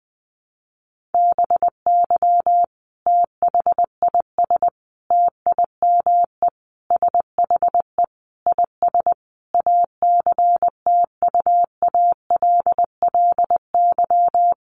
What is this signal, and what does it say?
Morse code, heard as a keyed tone, about 20 wpm, 700 Hz. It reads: BY THIS TIME SHE IS ACTUALLY
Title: Sentences from Top 700 Words 20wpm